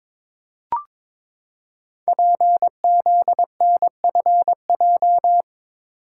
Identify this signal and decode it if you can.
Morse code, heard as a keyed tone, about 22 wpm, 700 Hz. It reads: PZNFJ
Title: Code Groups - 5 Letters - 22wpm